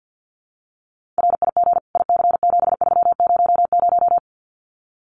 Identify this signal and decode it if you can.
Morse code, heard as a keyed tone, about 48 wpm, 700 Hz. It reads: RIG IC7300
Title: Common QSO Phrases Encore 50wpm